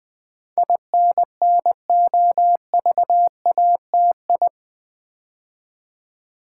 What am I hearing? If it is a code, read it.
Morse code, heard as a keyed tone, about 20 wpm, 700 Hz. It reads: INNOVATI
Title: Build A Word - 4 Letters At A Time - 10-Letter Words - 20wpm